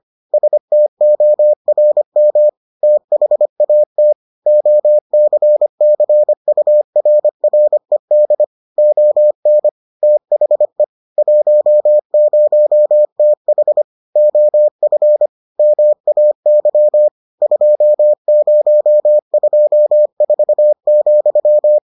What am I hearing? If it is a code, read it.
Morse code, heard as a keyed tone, about 25 wpm, 600 Hz. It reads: STORM THAT OCCURRED ON THE 10TH OF MAY 2024,